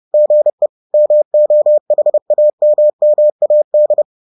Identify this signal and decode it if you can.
Morse code, heard as a keyed tone, about 30 wpm, 600 Hz. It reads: GE MOHAMMAD